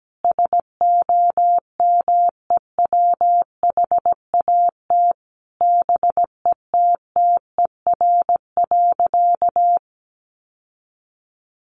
Morse code, heard as a keyed tone, about 17 wpm, 700 Hz. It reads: SOMEWHAT BETTER.
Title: Sentences from Top 1300 Words 17wpm